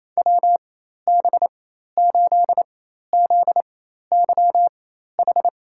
Morse code, heard as a keyed tone, about 28 wpm, spaced as for 13 wpm, 700 Hz. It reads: WB8ZYH
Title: USA Call Signs - 6 Characters - 4x Character Spacing - 28wpm